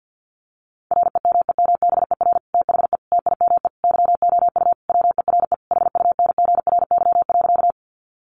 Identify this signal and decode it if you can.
Morse code, heard as a keyed tone, about 50 wpm, 700 Hz. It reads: REMEMBER THE TIME YOU WERE HUNGRY.
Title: Sentences from Top 1400 Words 50wpm